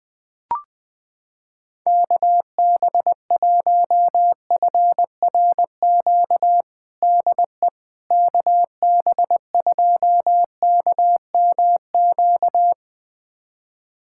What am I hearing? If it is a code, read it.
Morse code, heard as a keyed tone, about 19 wpm, 700 Hz. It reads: KB1FRQ DE KB2KMQ
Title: USA Callsign DE Callsign - 6 Characters - 20wpm